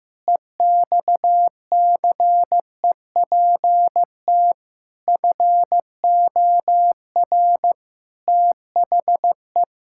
Morse code, heard as a keyed tone, about 15 wpm, 700 Hz. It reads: EXCEPT FOR THE
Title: Pangram Challenge v2 - 15wpm